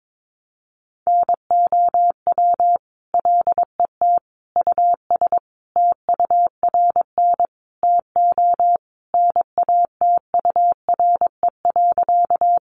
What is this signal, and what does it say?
Morse code, heard as a keyed tone, about 22 wpm, 700 Hz. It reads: NOW LET US TURN TO NATURE.
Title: Sentences from Top 600 Words 22wpm